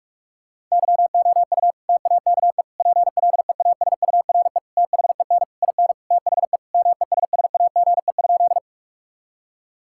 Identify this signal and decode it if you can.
Morse code, heard as a keyed tone, about 46 wpm, 700 Hz. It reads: YOU TAKE PLEASURE THEN IN THE MESSAGE?
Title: Sentences from Top 1600 Words 45wpm